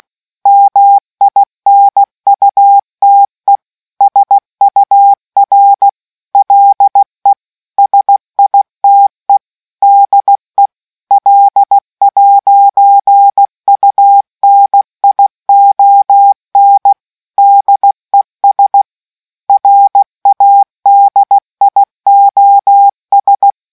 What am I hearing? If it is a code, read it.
Morse code, heard as a keyed tone, about 16 wpm, 800 Hz. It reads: MINUTE SUR LE SITE DE L'UNION DES RADIOS